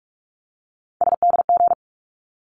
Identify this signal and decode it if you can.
Morse code, heard as a keyed tone, about 44 wpm, 700 Hz. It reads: SDG
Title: ARRL Sections 45wpm